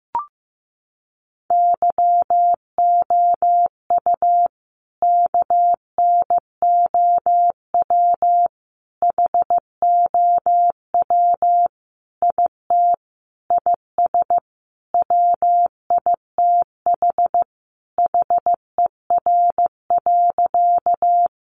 Morse code, heard as a keyed tone, about 15 wpm, 700 Hz. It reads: YOU KNOW HOW IT IS WITH HER.
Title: Sentences from Top 100 Words 15wpm